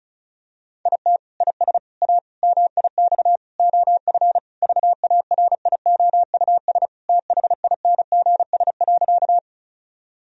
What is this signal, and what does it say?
Morse code, heard as a keyed tone, about 36 wpm, 700 Hz. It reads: IT IS A MIX OF VARIOUS THINGS.
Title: Sentences from Top 1000 Words - Review Part 4 - 35wpm